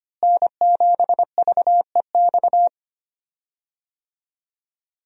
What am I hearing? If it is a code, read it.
Morse code, heard as a keyed tone, about 25 wpm, 700 Hz. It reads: N7VEX
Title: USA Call Signs - 5 Characters - Repeated 2x - 25wpm